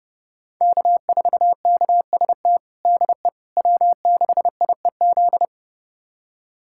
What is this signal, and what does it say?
Morse code, heard as a keyed tone, about 30 wpm, 700 Hz. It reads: K4KST DE W6IEZ
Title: USA Callsign DE Callsign - 5 Characters - 30wpm